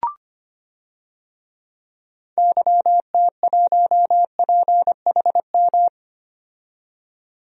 Morse code, heard as a keyed tone, about 25 wpm, 700 Hz. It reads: YT1PHM
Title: DXCC-100 Call Signs - 6 Characters - 25wpm